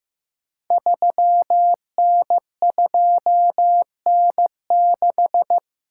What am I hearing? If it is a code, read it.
Morse code, heard as a keyed tone, about 15 wpm, 700 Hz. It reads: 3N2N6
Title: Code Groups - 5 Letter:Numbers - 15wpm